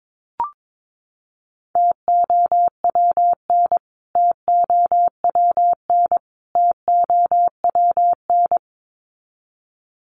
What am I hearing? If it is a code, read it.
Morse code, heard as a keyed tone, about 22 wpm, 700 Hz. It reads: TOWN TOWN TOWN
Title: Top 500 Words Repeated 22wpm